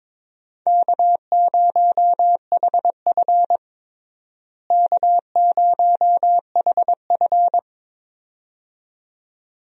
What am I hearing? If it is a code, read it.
Morse code, heard as a keyed tone, about 22 wpm, 700 Hz. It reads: K0HF K0HF
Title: USA Call Signs - 4 Characters - Repeated 2x - 22wpm